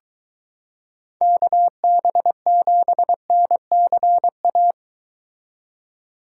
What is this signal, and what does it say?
Morse code, heard as a keyed tone, about 23 wpm, 700 Hz. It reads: KB7NCA